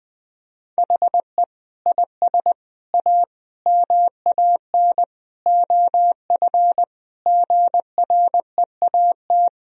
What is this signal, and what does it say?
Morse code, heard as a keyed tone, about 20 wpm, 700 Hz. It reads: HE IS A MAN OF GREAT
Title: Sentences from Top 1000 Words - Review Part 3 - 20wpm